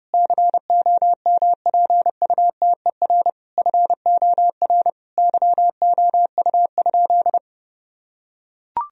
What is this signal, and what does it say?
Morse code, heard as a keyed tone, about 30 wpm, 700 Hz. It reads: COMPUTER FOR YOU?
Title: Sentences from Top 1000 Words - Review Part 3 - 30wpm